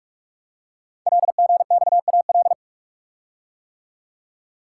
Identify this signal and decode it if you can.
Morse code, heard as a keyed tone, about 44 wpm, 700 Hz. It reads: LGXAL
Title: Code Groups - 5 Letters - 45wpm